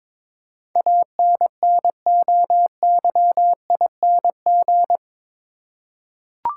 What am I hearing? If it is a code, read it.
Morse code, heard as a keyed tone, about 22 wpm, 700 Hz. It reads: ANNOYING
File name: practice / 22wpm / Eight Letter Words - 22wpm